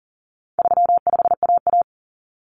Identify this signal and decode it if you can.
Morse code, heard as a keyed tone, about 39 wpm, 700 Hz. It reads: 35AA
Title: Common Cut Numbers - 4-Digit 40wpm